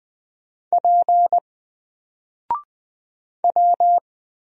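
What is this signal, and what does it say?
Morse code, heard as a keyed tone, about 20 wpm, 700 Hz. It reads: P W
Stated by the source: Single letter and number Morse drill at 20 wpm, from N4LCD.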